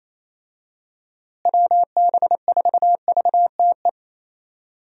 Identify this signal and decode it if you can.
Morse code, heard as a keyed tone, about 28 wpm, 700 Hz. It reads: WB4VTE
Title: USA Call Signs - 6 Characters - Repeated 3x - 28wpm